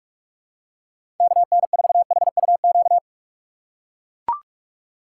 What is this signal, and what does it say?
Morse code, heard as a keyed tone, about 42 wpm, 690 Hz. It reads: KN4HUQ
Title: USA Call Signs - 6 Characters - 3x Character Spacing - 45wpm